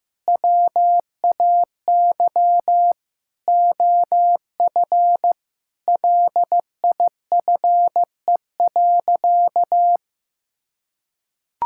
Morse code, heard as a keyed tone, about 15 wpm, 700 Hz. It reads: WAY OF LIFE.
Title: Sentences from Top 1000 Words - Review Part 6 - 15wpm